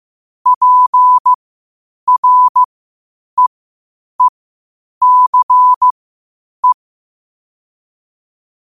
Morse code, heard as a keyed tone, about 15 wpm, spaced as for 8 wpm, 1000 Hz. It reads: PREECE
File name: qotdfiles / 8wpm